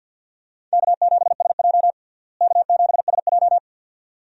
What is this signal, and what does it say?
Morse code, heard as a keyed tone, about 50 wpm, 700 Hz. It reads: K7SJ K7SJ
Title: USA Call Signs - 4 Characters - Repeated 2x - 50wpm